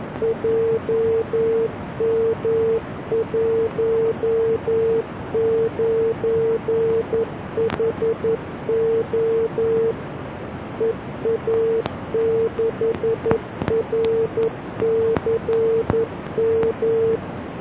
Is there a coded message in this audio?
Yes: Morse code from 0.2 to 17.2 s, heard as a keyed tone, about 11 wpm, 450 Hz. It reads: JM19HO EA6RCM